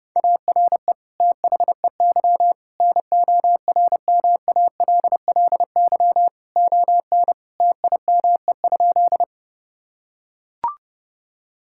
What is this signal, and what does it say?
Morse code, heard as a keyed tone, about 31 wpm, 700 Hz. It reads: ARE THEY NORMALLY ON TIME?
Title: Sentences from Top 1000 Words - Review Part 6 - 30wpm